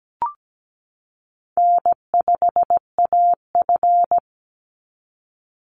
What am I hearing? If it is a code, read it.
Morse code, heard as a keyed tone, about 17 wpm, 700 Hz. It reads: N5AF
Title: Easy Call Signs - 17wpm